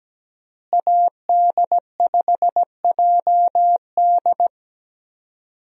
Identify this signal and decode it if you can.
Morse code, heard as a keyed tone, about 17 wpm, 700 Hz. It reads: AD5JD
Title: USA Call Signs - 5 Characters - Repeated 2x - 17wpm